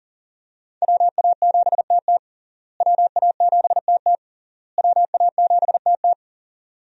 Morse code, heard as a keyed tone, about 40 wpm, 690 Hz. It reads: WA7TT WA7TT WA7TT
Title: USA Call Signs - 5 Characters - Repeated 3x - 40wpm